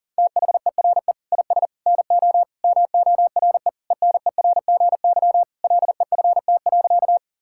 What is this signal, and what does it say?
Morse code, heard as a keyed tone, about 40 wpm, 700 Hz. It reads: THERE IS NO MORE ENERGY LEFT.
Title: Sentences from Top 1000 Words - Review Part 6 - 40wpm